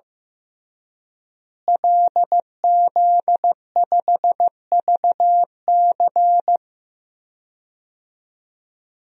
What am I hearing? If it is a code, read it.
Morse code, heard as a keyed tone, about 15 wpm, 700 Hz. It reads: LZ5VC
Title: DXCC-100 Call Signs - 5 Characters - Repeated 3x - 15wpm